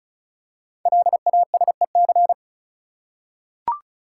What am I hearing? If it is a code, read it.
Morse code, heard as a keyed tone, about 36 wpm, 700 Hz. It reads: LASEC
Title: Code Groups - 5 Letters - 35wpm